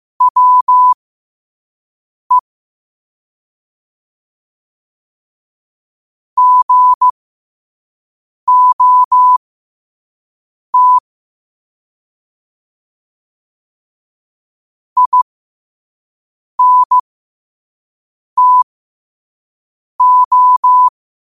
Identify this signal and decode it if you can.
Morse code, heard as a keyed tone, about 15 wpm, spaced as for 5 wpm, 1000 Hz. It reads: WE GOT INTO
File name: qotdfiles / 5wpm